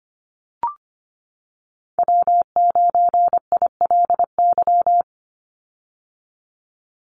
Morse code, heard as a keyed tone, about 25 wpm, 700 Hz. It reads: W9ILY
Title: Call Signs - 25wpm